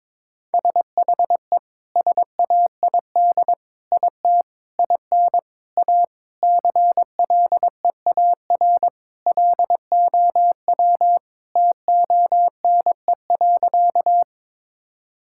Morse code, heard as a keyed tone, about 22 wpm, 700 Hz. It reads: SHE SAID IT IN A CLEAR LOW TONE.
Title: Sentences from Top 1600 Words 22wpm